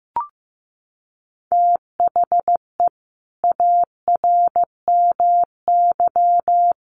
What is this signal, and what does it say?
Morse code, heard as a keyed tone, about 14 wpm, 700 Hz. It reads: THE ARMY
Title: Sentences from Top 800 Words 15wpm